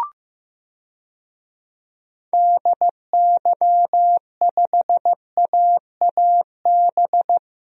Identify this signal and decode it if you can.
Morse code, heard as a keyed tone, about 15 wpm, 700 Hz. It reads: DY5AAB
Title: DXCC-100 Call Signs - 6 Characters - Repeated 2x - 15wpm